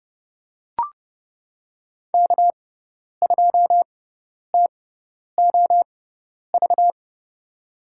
Morse code, heard as a keyed tone, about 30 wpm, spaced as for 10 wpm, 700 Hz. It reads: K2TOV
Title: USA Call Signs - 5 Characters - 6x Character Spacing - 30wpm